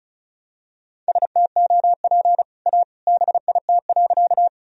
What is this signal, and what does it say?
Morse code, heard as a keyed tone, about 34 wpm, 700 Hz. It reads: STOP A BIT.
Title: Sentences from Top 1000 Words - Review Part 1 - 35wpm